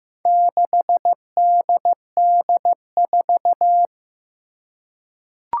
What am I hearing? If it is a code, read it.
Morse code, heard as a keyed tone, about 16 wpm, 700 Hz. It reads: 6DD4
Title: Four Challenging Letter-Number Combinations 15wpm